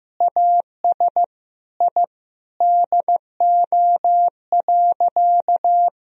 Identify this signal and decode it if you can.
Morse code, heard as a keyed tone, about 15 wpm, 700 Hz. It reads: AS I DO.
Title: Sentences from Top 1000 Words - Review Part 1 - 15wpm